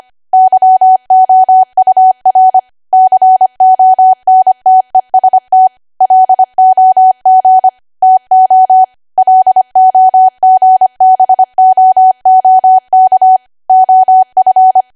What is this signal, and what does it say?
Morse code, heard as a keyed tone, about 25 wpm, 750 Hz. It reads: YOUR CONTEST LOG TO LOGBOOK OF